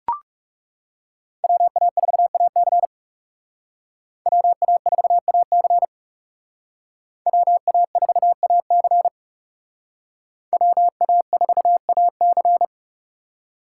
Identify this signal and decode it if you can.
Morse code, heard as a keyed tone, about 36 wpm, 700 Hz. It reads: WA4AC WA4AC WA4AC WA4AC
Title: Speed Racing - USA 2x2 Call Signs 30wpm